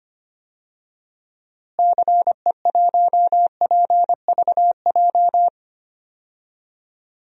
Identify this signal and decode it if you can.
Morse code, heard as a keyed tone, about 25 wpm, 700 Hz. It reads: CE1PVJ
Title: DXCC-100 Call Signs - 6 Characters - Repeated 3x - 25wpm